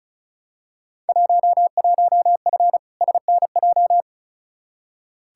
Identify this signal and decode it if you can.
Morse code, heard as a keyed tone, about 34 wpm, 700 Hz. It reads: 11F SNJ
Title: Field Day Exchanges Encore - 35wpm